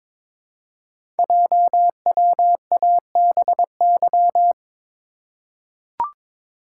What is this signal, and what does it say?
Morse code, heard as a keyed tone, about 22 wpm, 700 Hz. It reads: JWABY
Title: Code Groups - 5 Letters - 22wpm